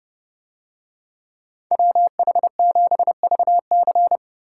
Morse code, heard as a keyed tone, about 29 wpm, 700 Hz. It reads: WH7VC